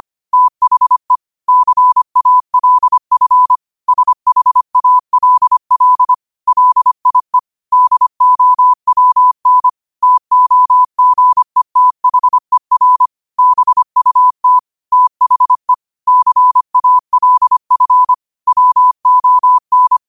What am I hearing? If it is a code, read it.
Morse code, heard as a keyed tone, about 25 wpm, 1000 Hz. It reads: THE CALF SHALL LIE DOWN TOGETHER BUT THE CALF WON
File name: qotdfiles / 25wpm